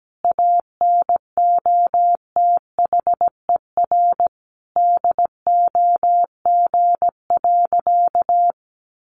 Morse code, heard as a keyed tone, about 17 wpm, 700 Hz. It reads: ANOTHER DOG.
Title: Easy Phrases 17wpm